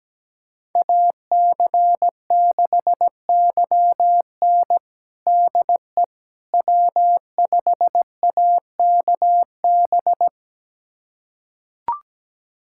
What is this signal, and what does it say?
Morse code, heard as a keyed tone, about 17 wpm, 700 Hz. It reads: AC6YN DE W5AKB
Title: USA Callsign DE Callsign - 5 Characters - 17wpm